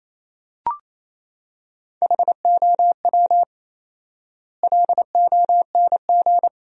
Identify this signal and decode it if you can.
Morse code, heard as a keyed tone, about 28 wpm, 700 Hz. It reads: HOW LONG
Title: 4x Word-Spacing - Sentences from Top 100 Words - 28wpm